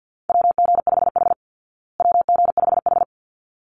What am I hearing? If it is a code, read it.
Morse code, heard as a keyed tone, about 49 wpm, 700 Hz. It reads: WG5H WG5H
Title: Call Signs Repeated 3x - 50wpm